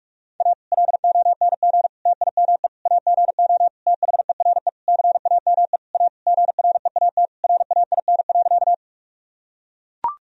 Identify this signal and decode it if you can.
Morse code, heard as a keyed tone, about 41 wpm, 700 Hz. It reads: A LONG TIME AGO THERE CAME A GREAT RAIN.
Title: Sentences from Top 1000 Words - Review Part 3 - 45wpm